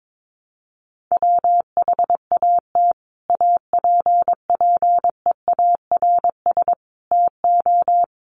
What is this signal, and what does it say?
Morse code, heard as a keyed tone, about 22 wpm, 700 Hz. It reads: WHAT APPEARS TO
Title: Sets of 5 Words Encore2 22wpm